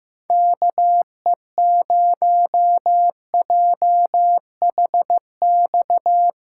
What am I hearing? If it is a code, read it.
Morse code, heard as a keyed tone, about 15 wpm, 700 Hz. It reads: KE0JHX
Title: USA Callsign DE Callsign - 6 Characters - 15wpm